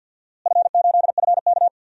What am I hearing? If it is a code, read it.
Morse code, heard as a keyed tone, about 51 wpm, 700 Hz. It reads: F8FK
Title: DXCC-100 Call Signs - 4 Characters - Repeated 2x - 50wpm